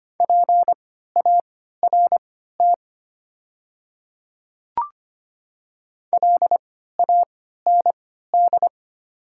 Morse code, heard as a keyed tone, about 25 wpm, spaced as for 14 wpm, 700 Hz. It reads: PART LAND
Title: 3x Character Spacing - Top 200 Words - 25wpm